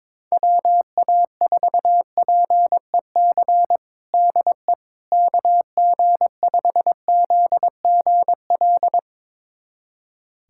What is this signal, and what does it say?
Morse code, heard as a keyed tone, about 22 wpm, 700 Hz. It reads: WA4PEC DE KG5ZGL
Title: USA Callsign DE Callsign - 6 Characters - 22wpm